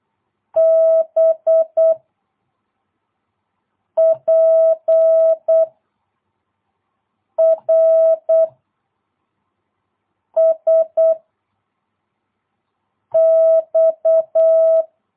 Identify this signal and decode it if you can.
Morse code, heard as a keyed tone, about 8 wpm, 660 Hz. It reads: B P R S X
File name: jh8cbh / 2022cw